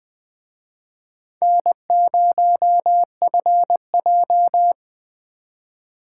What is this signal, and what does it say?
Morse code, heard as a keyed tone, about 20 wpm, 700 Hz. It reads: N0FJ
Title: Speed Racing - USA Call Signs 17wpm